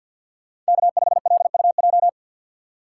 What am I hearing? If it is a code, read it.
Morse code, heard as a keyed tone, about 63 wpm, 700 Hz. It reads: K5LUJ